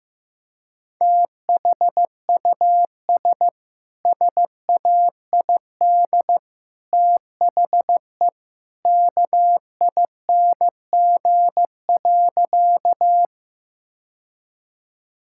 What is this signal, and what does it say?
Morse code, heard as a keyed tone, about 15 wpm, 700 Hz. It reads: THUS SAID THE KING.
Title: Sentences from Top 1000 Words - Review Part 5 - 15wpm